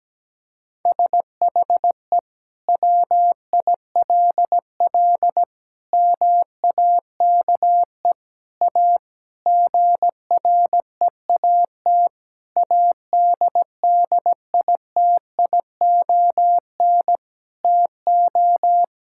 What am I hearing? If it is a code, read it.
Morse code, heard as a keyed tone, about 17 wpm, 700 Hz. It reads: SHE WILL MAKE A GREAT ADDITION TO
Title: Sentences from Top 1000 Words - Review Part 6 - 17wpm